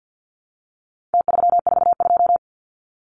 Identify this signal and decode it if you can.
Morse code, heard as a keyed tone, about 50 wpm, 700 Hz. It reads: T342
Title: Numbers - Contest Style - 4-Digit 50wpm